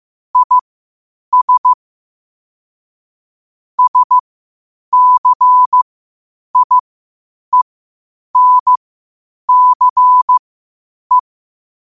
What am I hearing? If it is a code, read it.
Morse code, heard as a keyed tone, about 15 wpm, spaced as for 8 wpm, 1000 Hz. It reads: IS SCIENCE
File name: qotdfiles / 8wpm